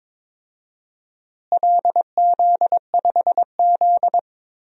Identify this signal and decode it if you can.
Morse code, heard as a keyed tone, about 22 wpm, 700 Hz. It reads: LZ5Z
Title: DXCC-100 Call Signs - 4 Characters - Repeated 3x - 22wpm